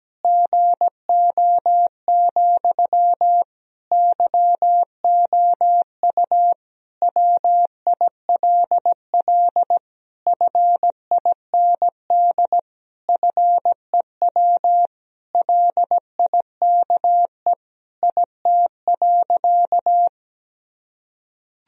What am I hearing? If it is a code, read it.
Morse code, heard as a keyed tone, about 17 wpm, 700 Hz. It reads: GO, YOU WILL FIND FEW LIKE IT.
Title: Sentences from Top 1000 Words - Review Part 1 - 17wpm